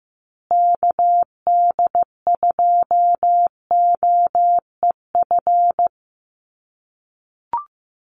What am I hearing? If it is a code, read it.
Morse code, heard as a keyed tone, about 15 wpm, 700 Hz. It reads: KD2OEF
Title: USA Call Signs Repeated 3x 15wpm